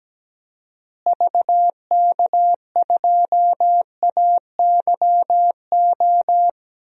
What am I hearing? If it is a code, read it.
Morse code, heard as a keyed tone, about 17 wpm, 700 Hz. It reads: VK2AYO
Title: DXCC-100 Call Signs - 6 Characters - Repeated 3x - 17wpm